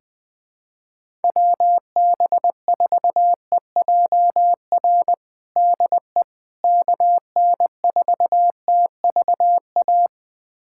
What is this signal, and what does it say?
Morse code, heard as a keyed tone, about 20 wpm, 700 Hz. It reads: WB4EJR DE KN4TVA
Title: USA Callsign DE Callsign 20wpm